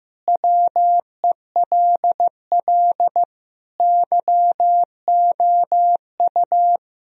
Morse code, heard as a keyed tone, about 15 wpm, 700 Hz. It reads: WELL YOU
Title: Sentences from Top 1000 Words - Review Part 1 - 15wpm